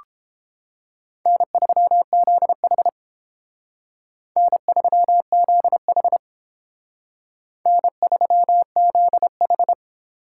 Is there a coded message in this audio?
Yes: Morse code from 1.3 to 9.7 s, heard as a keyed tone, about 29 wpm, 700 Hz. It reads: N3ZH N3ZH N3ZH